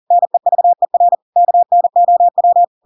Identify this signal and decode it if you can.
Morse code, heard as a keyed tone, about 40 wpm, 700 Hz. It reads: NEVER KNOW
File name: Pinocchio / Pinocchio ch7 40 40 ik1hge